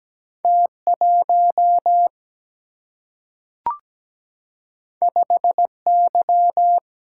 Morse code, heard as a keyed tone, about 17 wpm, 700 Hz. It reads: T1 5Y
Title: Single-Letter Single-Number 17wpm